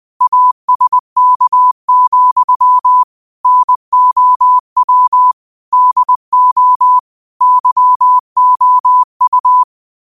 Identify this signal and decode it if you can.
Morse code, heard as a keyed tone, about 20 wpm, 1000 Hz. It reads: ASK, NOW DO YOU
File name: qotdfiles / 20wpm